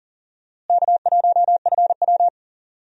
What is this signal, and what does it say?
Morse code, heard as a keyed tone, about 38 wpm, 700 Hz. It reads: K1FW
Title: USA Call Signs - 4 Characters - Repeated 3x - 40wpm